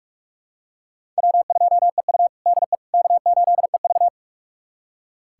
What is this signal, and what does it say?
Morse code, heard as a keyed tone, about 41 wpm, 700 Hz. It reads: W2EU DE K8EV
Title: USA Callsign DE Callsign - 4 Characters - 45wpm